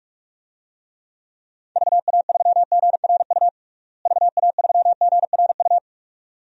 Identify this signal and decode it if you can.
Morse code, heard as a keyed tone, about 46 wpm, 700 Hz. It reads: VA3GRU VA3GRU
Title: DXCC-100 Call Signs - 6 Characters - Repeated 2x - 45wpm